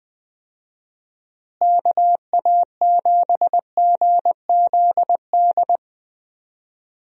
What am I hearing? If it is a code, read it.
Morse code, heard as a keyed tone, about 20 wpm, 700 Hz. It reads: KA7GZD
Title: USA Call Signs - 6 Characters - Repeated 2x - 20wpm